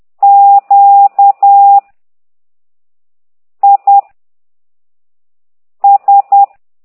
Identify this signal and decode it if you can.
Morse code, heard as a keyed tone, about 10 wpm, 800 Hz. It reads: Q I S
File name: ct2idw / exerc15